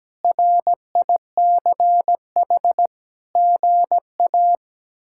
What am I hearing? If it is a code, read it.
Morse code, heard as a keyed tone, about 17 wpm, 700 Hz. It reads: RICH GA
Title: K1USN SST QSO Practice - 17wpm